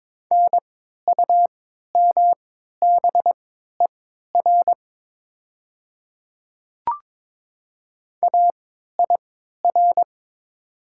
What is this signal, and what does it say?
Morse code, heard as a keyed tone, about 22 wpm, spaced as for 12 wpm, 700 Hz. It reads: NUMBER AIR
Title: 3x Character Spacing - Top 200 Words - 22wpm